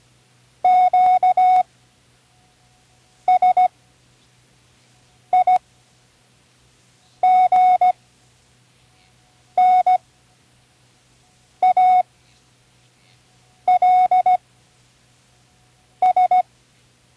Morse code, heard as a keyed tone, about 16 wpm, spaced as for 5 wpm, 740 Hz. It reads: Q SIGNALS
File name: Learning Morse Code (from Tune in the World)